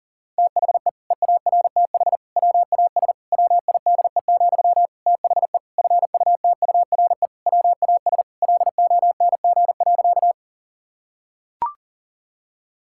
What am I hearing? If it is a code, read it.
Morse code, heard as a keyed tone, about 40 wpm, 700 Hz. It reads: THE EARTH WAS WIDE, THE FUTURE WAS LONG.
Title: Sentences from Top 1000 Words - Review Part 6 - 40wpm